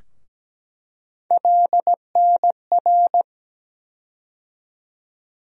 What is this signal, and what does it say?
Morse code, heard as a keyed tone, about 17 wpm, 700 Hz. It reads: LNR
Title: USA Call Sign Postfixes - 3 Letters 17wpm